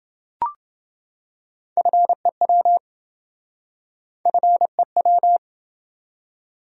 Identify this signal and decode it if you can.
Morse code, heard as a keyed tone, about 28 wpm, 700 Hz. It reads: FEW FEW